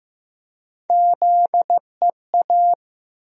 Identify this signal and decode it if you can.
Morse code, heard as a keyed tone, about 15 wpm, 700 Hz. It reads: ZEA
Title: USA Call Sign Postfixes - 3 Letters 15wpm